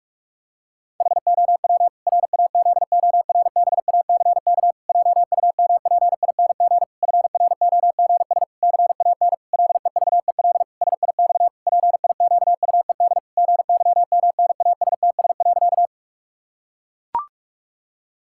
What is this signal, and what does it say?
Morse code, heard as a keyed tone, about 41 wpm, 700 Hz. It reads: HOW RAZORBACK JUMPING FROGS CAN LEVEL SIX PIQUED GYMNASTS.
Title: Pangram Challenge v2 - 45wpm